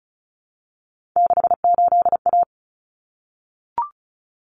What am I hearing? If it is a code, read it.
Morse code, heard as a keyed tone, about 34 wpm, 700 Hz. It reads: 68A
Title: Common Cut Numbers - 3-Digit 35wpm